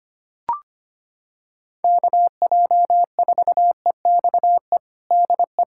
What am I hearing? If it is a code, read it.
Morse code, heard as a keyed tone, about 25 wpm, 700 Hz. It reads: KJ4EXE DE
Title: USA Callsign DE Callsign - 6 Characters - 25wpm